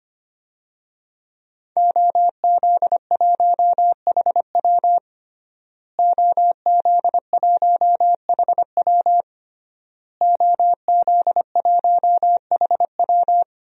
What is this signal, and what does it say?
Morse code, heard as a keyed tone, about 25 wpm, 700 Hz. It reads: OZ1HW OZ1HW OZ1HW
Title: DXCC-100 Call Signs - 5 Characters - Repeated 3x - 25wpm